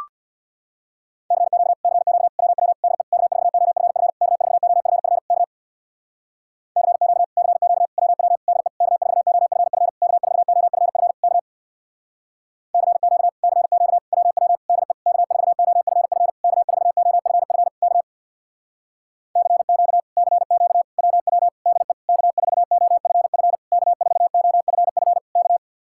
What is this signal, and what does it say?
Morse code, heard as a keyed tone, about 64 wpm, 700 Hz. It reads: CQ CQ WW DE K4OFF K4OFF K CQ CQ WW DE K4OFF K4OFF K CQ CQ WW DE K4OFF K4OFF K CQ CQ WW DE K4OFF K4OFF K